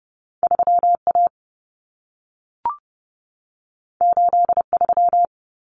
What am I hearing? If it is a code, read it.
Morse code, heard as a keyed tone, about 30 wpm, 700 Hz. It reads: 3A 83
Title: Common Cut Numbers - 2-Digit 30wpm